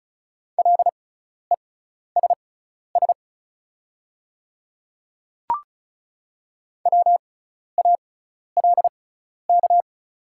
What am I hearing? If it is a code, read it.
Morse code, heard as a keyed tone, about 34 wpm, spaced as for 12 wpm, 700 Hz. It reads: LESS WALK
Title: 6x Character Spacing - Top 500 Words - 35wpm